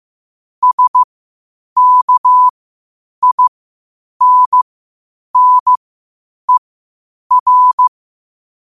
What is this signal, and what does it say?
Morse code, heard as a keyed tone, about 15 wpm, spaced as for 8 wpm, 1000 Hz. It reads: SKINNER
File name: qotdfiles / 8wpm